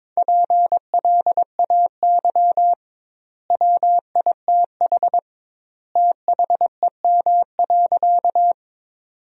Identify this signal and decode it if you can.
Morse code, heard as a keyed tone, about 23 wpm, 700 Hz. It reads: PLAY WITH THEM.
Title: 2x Word-Spacing - Sentences from Top 200 Words - 22wpm